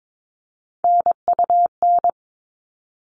Morse code, heard as a keyed tone, about 22 wpm, 700 Hz. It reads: NUN
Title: All Cut Numbers - 3-Digit 22wpm